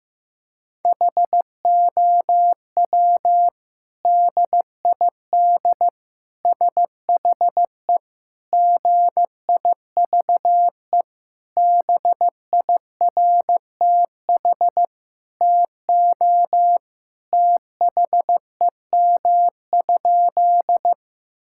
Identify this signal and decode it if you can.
Morse code, heard as a keyed tone, about 16 wpm, 700 Hz. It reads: HOW DID SHE GIVE BIRTH TO THEM?
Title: Sentences from Top 1000 Words - Review Part 5 - 15wpm